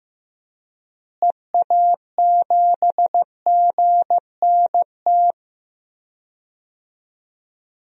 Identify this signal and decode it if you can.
Morse code, heard as a keyed tone, about 15 wpm, 700 Hz. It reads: EA7GNT
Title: DXCC-100 Call Signs - 6 Characters - Repeated 3x - 15wpm